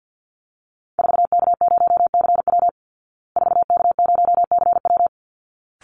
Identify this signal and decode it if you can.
Morse code, heard as a keyed tone, about 50 wpm, 700 Hz. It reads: 4K0CW 4K0CW
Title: Call Signs Repeated 3x - 50wpm